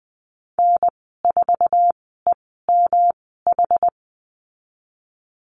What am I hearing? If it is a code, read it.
Morse code, heard as a keyed tone, about 20 wpm, spaced as for 14 wpm, 700 Hz. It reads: N4EMH
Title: 186 Lesson 30g - Letter b - Call Signs 20wpm